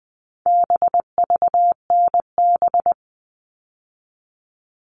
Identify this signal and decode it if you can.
Morse code, heard as a keyed tone, about 20 wpm, 700 Hz. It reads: BVNB
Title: All Cut Numbers - 4-Digit 20wpm